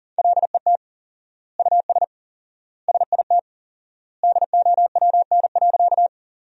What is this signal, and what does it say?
Morse code, heard as a keyed tone, about 40 wpm, 700 Hz. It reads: LET US SIT DOWN.
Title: 4x Word-Spacing - Sentences from Top 500 Words - 40wpm